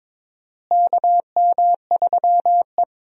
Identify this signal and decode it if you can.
Morse code, heard as a keyed tone, about 22 wpm, 700 Hz. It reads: KM3E
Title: USA Call Signs - 4 Characters - Repeated 3x - 22wpm